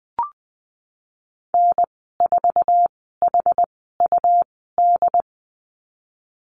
Morse code, heard as a keyed tone, about 20 wpm, spaced as for 15 wpm, 700 Hz. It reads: N4HUD
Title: 128 Lesson 21g - Number 6 - Call Signs 20wpm